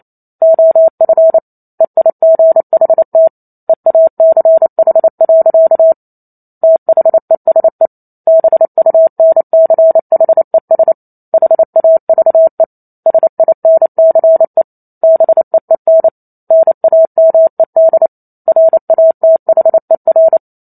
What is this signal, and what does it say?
Morse code, heard as a keyed tone, about 28 wpm, 650 Hz. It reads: OF EIGHT EACH. THESE BUNCHES HAVE SINCE BEEN NAMED RATHER
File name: Chapter 2 - 28 WPM